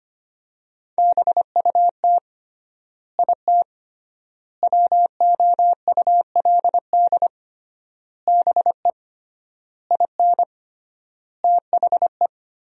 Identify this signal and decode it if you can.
Morse code, heard as a keyed tone, about 25 wpm, 700 Hz. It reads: BUT IT WOULD BE IN THE
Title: 3x Word-Spacing - Sentences from Top 300 Words - 25wpm